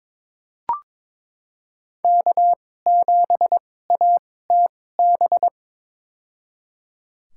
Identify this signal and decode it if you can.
Morse code, heard as a keyed tone, about 22 wpm, spaced as for 16 wpm, 700 Hz. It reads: K7ATB
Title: USA Call Signs - 5 Characters - 2x Character Spacing - 22wpm